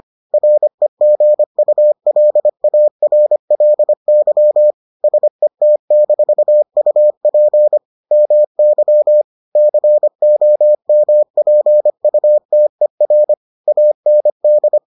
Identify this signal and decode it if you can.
Morse code, heard as a keyed tone, about 25 wpm, 600 Hz. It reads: REGULARLY SET-UP MY COMPUTER AND